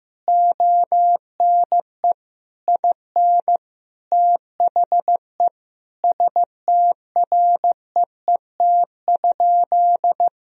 Morse code, heard as a keyed tone, about 14 wpm, 700 Hz. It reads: ONE IN THE STREET?
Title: Sentences from Top 1000 Words - Review Part 3 - 15wpm